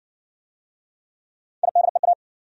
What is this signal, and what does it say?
Morse code, heard as a keyed tone, about 56 wpm, 700 Hz. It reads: IDEA